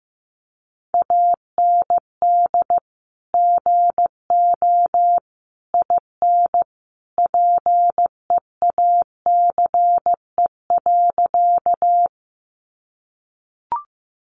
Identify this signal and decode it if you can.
Morse code, heard as a keyed tone, about 15 wpm, 700 Hz. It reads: AND GO IN PEACE.
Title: Sentences from Top 1500 Words 15wpm